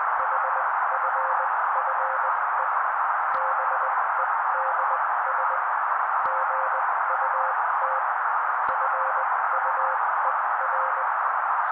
Morse code, heard as a keyed tone, about 20 wpm, spaced as for 16 wpm, 530 Hz. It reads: HFFE BEDS GUT FUER
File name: daily / 16wpm noise